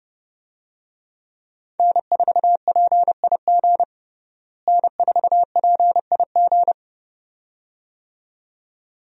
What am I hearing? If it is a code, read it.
Morse code, heard as a keyed tone, about 31 wpm, 700 Hz. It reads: N4PIG N4PIG
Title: USA Call Signs - 5 Characters - Repeated 2x - 30wpm